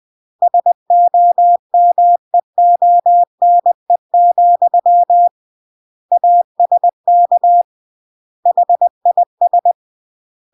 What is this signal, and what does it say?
Morse code, heard as a keyed tone, about 20 wpm, 700 Hz. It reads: SOMEONE, ASK HIS
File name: CW / jodorowsky / 82 commandments 20-20